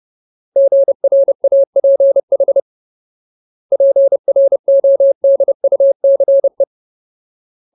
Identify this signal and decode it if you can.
Morse code, heard as a keyed tone, about 30 wpm, 550 Hz. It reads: GRAPH PRODUCE
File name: Ultimate Word List / 30WPM Ultimate